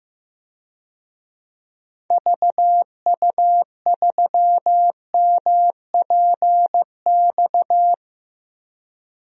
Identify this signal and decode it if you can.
Morse code, heard as a keyed tone, about 15 wpm, 700 Hz. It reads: VU3MPX
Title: DXCC-100 Call Signs - 6 Characters - Repeated 2x - 15wpm